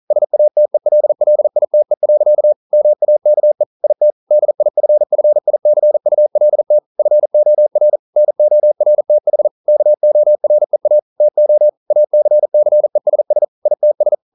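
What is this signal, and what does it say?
Morse code, heard as a keyed tone, about 42 wpm, 600 Hz. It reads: SATELLITE. MAKE IT DIFFICULT FOR NORTH KOREA TO ACCESS ITS
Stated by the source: Morse code practice at 41 wpm, from CW med Gnister.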